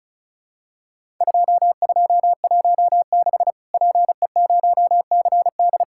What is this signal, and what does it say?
Morse code, heard as a keyed tone, about 34 wpm, 700 Hz. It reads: 2216 PE0CD
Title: Worked All Europe DX Contest - QTC Exchange - 35wpm